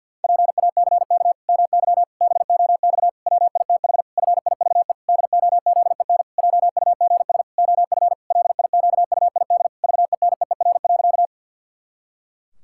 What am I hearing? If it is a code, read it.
Morse code, heard as a keyed tone, about 50 wpm, 700 Hz. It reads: PACK MY BOX WITH FIVE DOZEN JUGS OF LIQUID VENEER.
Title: Pangram Challenge v2 - 50wpm